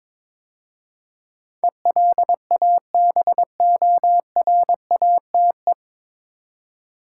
Morse code, heard as a keyed tone, about 22 wpm, 700 Hz. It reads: ELABORATE
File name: 9 Letter Words - First 3 Letters Given - 22wpm